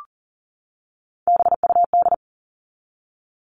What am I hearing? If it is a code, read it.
Morse code, heard as a keyed tone, about 40 wpm, 700 Hz. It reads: BUD
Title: All Cut Numbers - 3-Digit 40wpm